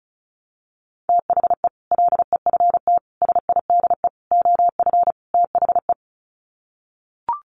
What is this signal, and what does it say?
Morse code, heard as a keyed tone, about 34 wpm, 700 Hz. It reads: THE LEFT SIDE OF THE
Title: Sets of 5 Words Encore 35wpm